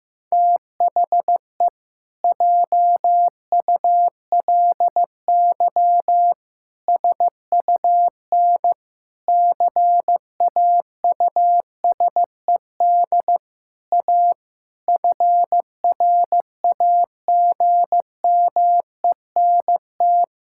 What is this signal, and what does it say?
Morse code, heard as a keyed tone, about 15 wpm, 700 Hz. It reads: THE JULY SUN CAUSED A FRAGMENT
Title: Pangram Challenge v2 - 15wpm